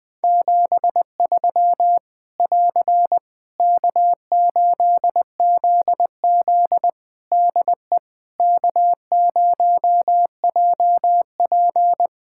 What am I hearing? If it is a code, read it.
Morse code, heard as a keyed tone, about 21 wpm, 700 Hz. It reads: 73 <AR> K8ZZ DE K0JP
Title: Practice QSO - Exchange 4 - Ending - 20wpm